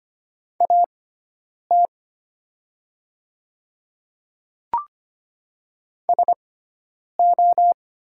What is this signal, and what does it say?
Morse code, heard as a keyed tone, about 25 wpm, spaced as for 9 wpm, 700 Hz. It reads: AT SO
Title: 6x Character Spacing - Top 100 Words - 25wpm